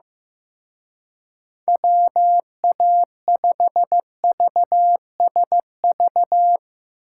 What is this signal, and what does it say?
Morse code, heard as a keyed tone, about 15 wpm, 700 Hz. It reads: WA5VSV
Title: Speed Racing - USA Call Signs 15wpm